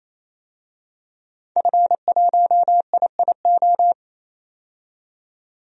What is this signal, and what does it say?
Morse code, heard as a keyed tone, about 28 wpm, 700 Hz. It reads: F1IIO
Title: DXCC-100 Call Signs - 5 Characters - 28wpm